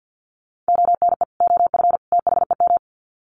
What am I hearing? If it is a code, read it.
Morse code, heard as a keyed tone, about 50 wpm, 700 Hz. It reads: ONE OF THEM
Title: Sets of 3 Words 50wpm